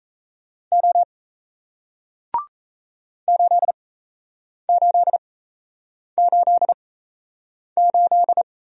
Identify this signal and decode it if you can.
Morse code, heard as a keyed tone, about 37 wpm, spaced as for 8 wpm, 700 Hz. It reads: O 8888